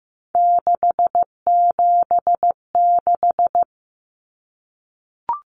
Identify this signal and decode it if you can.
Morse code, heard as a keyed tone, about 15 wpm, 700 Hz. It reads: 676
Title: Numbers - 3-Digit 15wpm